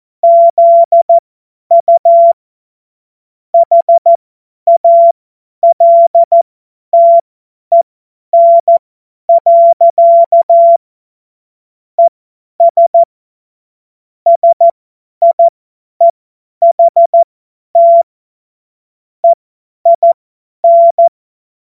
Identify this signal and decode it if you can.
Morse code, heard as a keyed tone, about 14 wpm, spaced as for 10 wpm, 680 Hz. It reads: ZU HALTEN. ES SIEHT EIN
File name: Morsekurs / Witze / Witz 11. Mai 2021, 50 F70